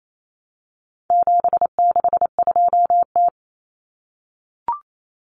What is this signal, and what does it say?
Morse code, heard as a keyed tone, about 28 wpm, 700 Hz. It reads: 762T
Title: Common Cut Numbers - 4-Digit 28wpm